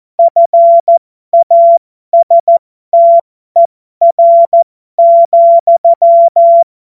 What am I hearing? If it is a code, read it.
Morse code, heard as a keyed tone, about 14 wpm, spaced as for 12 wpm, 680 Hz. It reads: FASTER,
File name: Morsekurs / Witze